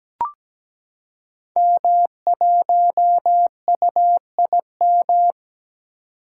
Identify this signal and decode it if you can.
Morse code, heard as a keyed tone, about 17 wpm, 700 Hz. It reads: M1UIM
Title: Code Groups - 5 Letter:Numbers - 17wpm